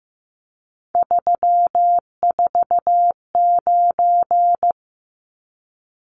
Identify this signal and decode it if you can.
Morse code, heard as a keyed tone, about 15 wpm, 700 Hz. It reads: 349
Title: Numbers - 3-Digit 15wpm